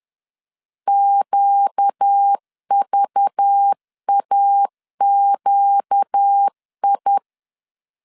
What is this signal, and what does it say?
Morse code, heard as a keyed tone, about 10 wpm, 800 Hz. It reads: QVAQI
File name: CW / DET / 1 6 1500C S B 10wpm 4137